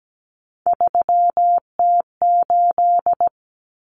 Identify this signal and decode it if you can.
Morse code, heard as a keyed tone, about 17 wpm, 700 Hz. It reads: 3T8
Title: Common Cut Numbers - 3-Digit 17wpm